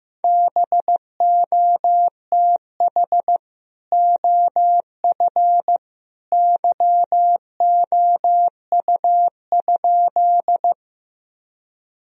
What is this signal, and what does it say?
Morse code, heard as a keyed tone, about 15 wpm, 700 Hz. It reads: BOTH OF YOU?
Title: Sentences from Top 1000 Words - Review Part 1 - 15wpm